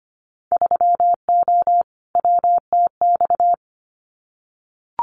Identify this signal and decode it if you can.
Morse code, heard as a keyed tone, about 25 wpm, 700 Hz. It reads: 3O WTX
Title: Winter Field Day Exchanges Encore 25wpm